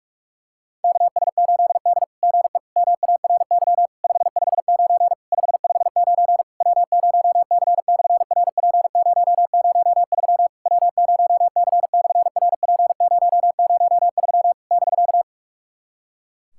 Morse code, heard as a keyed tone, about 46 wpm, 700 Hz. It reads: KS8D GE MARY 559 559 W0C/RP003 W0C/RP003 <BK>
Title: SOTA QSO Practice 45wpm